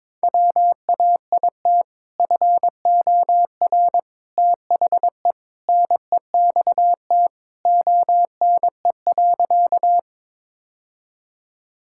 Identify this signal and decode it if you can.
Morse code, heard as a keyed tone, about 22 wpm, 700 Hz. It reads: WAIT FOR THE NEXT ONE.
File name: practice / 22wpm / Sentences with 2-Word Phrases - 22wpm